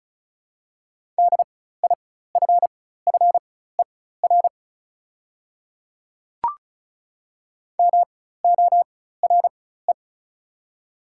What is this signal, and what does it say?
Morse code, heard as a keyed tone, about 36 wpm, spaced as for 16 wpm, 700 Hz. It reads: DIFFER MORE
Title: 4x Character Spacing - Top 200 Words - 35wpm